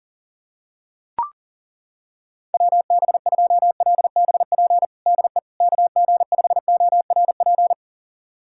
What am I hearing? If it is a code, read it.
Morse code, heard as a keyed tone, about 40 wpm, 700 Hz. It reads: WB2LBP DE KG5ORP
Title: USA Callsign DE Callsign - 6 Characters - 40wpm